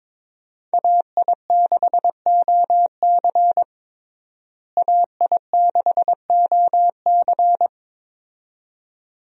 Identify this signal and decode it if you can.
Morse code, heard as a keyed tone, about 22 wpm, 700 Hz. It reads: AI6OC AI6OC
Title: USA Call Signs - 5 Characters - Repeated 2x - 22wpm